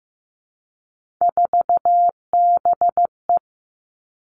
Morse code, heard as a keyed tone, about 15 wpm, 700 Hz. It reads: 4BE
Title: Three Challenging Letter-Number Combinations 15wpm